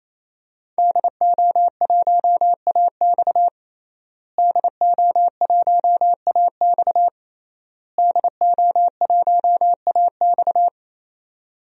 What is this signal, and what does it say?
Morse code, heard as a keyed tone, about 28 wpm, 700 Hz. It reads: DO1AX DO1AX DO1AX
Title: DXCC-100 Call Signs - 5 Characters - Repeated 3x - 28wpm